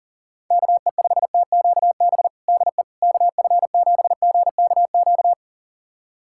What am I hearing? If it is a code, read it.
Morse code, heard as a keyed tone, about 40 wpm, 700 Hz. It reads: KE5TQB DE KF7GKQ
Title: USA Callsign DE Callsign - 6 Characters - 40wpm